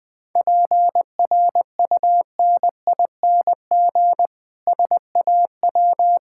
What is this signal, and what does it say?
Morse code, heard as a keyed tone, about 20 wpm, 700 Hz. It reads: PRUNING SAW
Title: Difficult Words - Speed-Racing with Hints - 20wpm